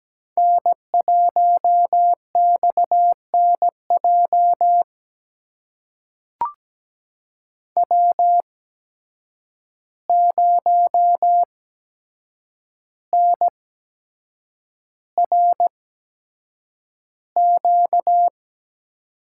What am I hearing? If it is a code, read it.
Morse code, heard as a keyed tone, about 17 wpm, spaced as for 7 wpm, 700 Hz. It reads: N1XNJ W0NRQ
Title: USA Call Signs - 5 Characters - 8x Character Spacing - 17wpm